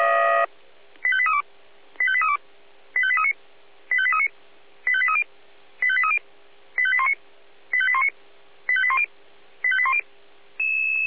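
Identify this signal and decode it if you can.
Five-tone selective call (ZVEI), heard as quick runs of five tones, tones of about 70 ms. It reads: 85832 85832 85839 85839 85830 85830 85819 85819 85810 85810
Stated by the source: ZVEI five-tone selective call 85278 received by SDR.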